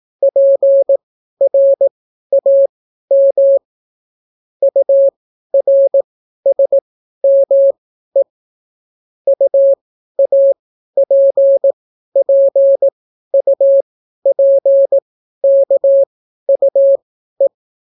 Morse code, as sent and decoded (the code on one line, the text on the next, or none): .--. .-. .- -- / ..- .-. ... -- . / ..- .- .--. .--. ..- .--. -.- ..- .
PRAM URSME UAPPUPKUE